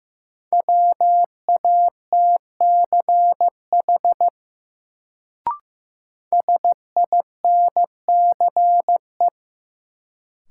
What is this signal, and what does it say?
Morse code, heard as a keyed tone, about 15 wpm, 700 Hz. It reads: WATCH SINCE
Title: Sentences from Top 300 Words - New Word Drill - 15wpm